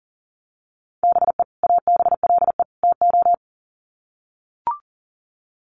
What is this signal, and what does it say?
Morse code, heard as a keyed tone, about 40 wpm, 700 Hz. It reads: BE ABLE TO
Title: Sets of 3 Words 40wpm